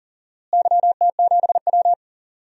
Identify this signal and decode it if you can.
Morse code, heard as a keyed tone, about 40 wpm, 700 Hz. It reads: YT7W
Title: DXCC-100 Call Signs - 4 Characters - Repeated 3x - 40wpm